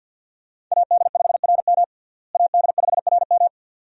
Morse code, heard as a keyed tone, about 51 wpm, 700 Hz. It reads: AD5RM AD5RM
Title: USA Call Signs - 5 Characters - Repeated 2x - 50wpm